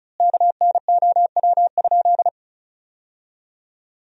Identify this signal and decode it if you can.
Morse code, heard as a keyed tone, about 34 wpm, 700 Hz. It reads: KNOW?
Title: 4x Word-Spacing - Sentences from Top 200 Words - 35wpm